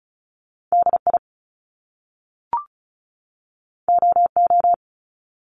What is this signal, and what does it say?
Morse code, heard as a keyed tone, about 36 wpm, 700 Hz. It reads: DI OO
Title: Top Two Letter Combinations in Words 35wpm